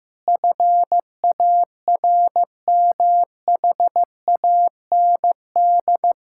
Morse code, heard as a keyed tone, about 15 wpm, 700 Hz. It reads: FARMHAND
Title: Pangram Challenge v2 - 15wpm